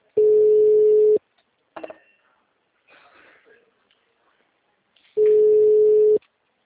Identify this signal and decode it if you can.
Ringback tone (Europe, 425 Hz).